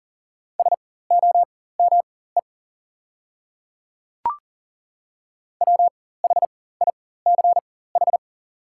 Morse code, heard as a keyed tone, about 40 wpm, spaced as for 19 wpm, 690 Hz. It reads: SOME WHICH
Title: 4x Character Spacing - Top 500 Words - 40wpm